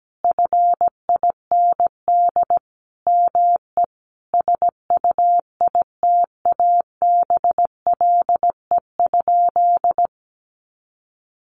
Morse code, heard as a keyed tone, about 17 wpm, 700 Hz. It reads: FIND ME SUITABLE?
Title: Sentences from Top 1400 Words 17wpm